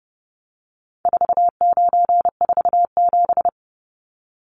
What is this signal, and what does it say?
Morse code, heard as a keyed tone, about 30 wpm, 700 Hz. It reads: 4947
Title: Numbers - 4-Digit 30wpm